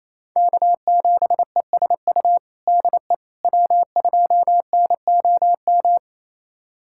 Morse code, heard as a keyed tone, about 28 wpm, 700 Hz. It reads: K7ESU DE W2NOM
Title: USA Callsign DE Callsign - 5 Characters - 28wpm